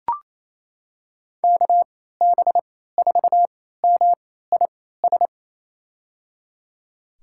Morse code, heard as a keyed tone, about 28 wpm, spaced as for 16 wpm, 700 Hz. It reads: KB4MIS
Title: USA Call Signs - 3x Character Spacing - 28wpm